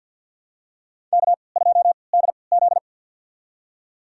Morse code, heard as a keyed tone, about 52 wpm, spaced as for 29 wpm, 700 Hz. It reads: K2DZ
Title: USA Call Signs - 3x Character Spacing - 50wpm